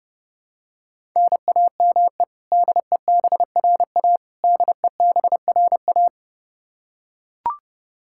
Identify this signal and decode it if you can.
Morse code, heard as a keyed tone, about 32 wpm, 700 Hz. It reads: NAME DEBRA DEBRA
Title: Practice QSO - Exchange 1 - Beginning - 30wpm